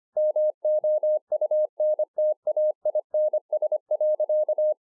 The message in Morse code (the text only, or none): MOUNTAINS.